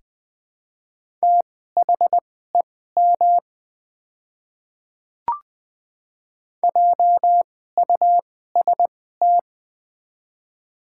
Morse code, heard as a keyed tone, about 20 wpm, spaced as for 14 wpm, 700 Hz. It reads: THEM JUST
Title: 2x Character Spacing - Top 200 Words - 20wpm